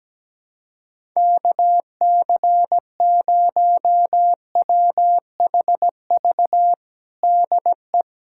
Morse code, heard as a keyed tone, about 17 wpm, 700 Hz. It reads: KC0WHV DE
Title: USA Callsign DE Callsign 17wpm